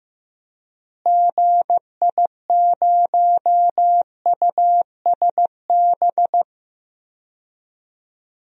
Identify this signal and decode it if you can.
Morse code, heard as a keyed tone, about 16 wpm, 700 Hz. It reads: GI0USB
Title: DXCC-100 Call Signs - 6 Characters - Repeated 3x - 15wpm